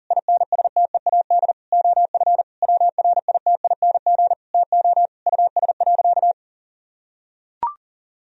Morse code, heard as a keyed tone, about 40 wpm, 700 Hz. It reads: INSTEAD OF WRITING TO US.
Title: Sentences from Top 1000 Words - Review Part 4 - 40wpm